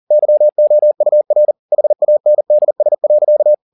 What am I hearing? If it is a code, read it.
Morse code, heard as a keyed tone, about 40 wpm, 600 Hz. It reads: YOUR HANDS.
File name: the old man and the sea 2 40wpm